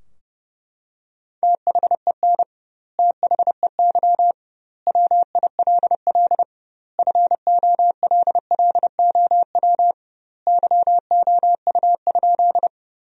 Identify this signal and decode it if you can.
Morse code, heard as a keyed tone, about 32 wpm, 700 Hz. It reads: THEN THEY WILL FOLLOW YOU?
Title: 2x Word-Spacing - Sentences from Top 200 Words - 30wpm